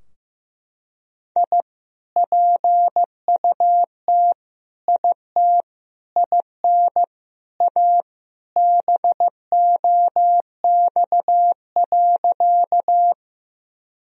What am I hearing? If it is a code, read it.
Morse code, heard as a keyed tone, about 15 wpm, 700 Hz. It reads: I PUT IT IN A BOX.
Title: Sentences from Top 1000 Words - Review Part 2 - 15wpm